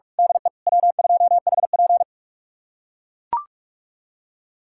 Morse code, heard as a keyed tone, about 43 wpm, 700 Hz. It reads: DE W2HP
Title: USA Callsign DE Callsign - 4 Characters - 45wpm